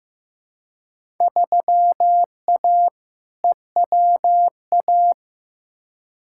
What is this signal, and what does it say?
Morse code, heard as a keyed tone, about 15 wpm, 700 Hz. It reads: 3A EWA
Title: Field Day Exchanges Encore - 15wpm